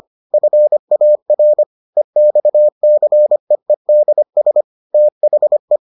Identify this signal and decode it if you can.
Morse code, heard as a keyed tone, about 25 wpm, 600 Hz. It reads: FAR EXCEEDS THE